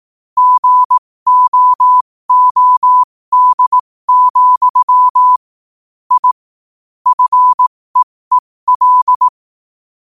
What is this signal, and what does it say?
Morse code, heard as a keyed tone, about 18 wpm, spaced as for 15 wpm, 1000 Hz. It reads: GOOD, I FEEL
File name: qotdfiles / 15wpm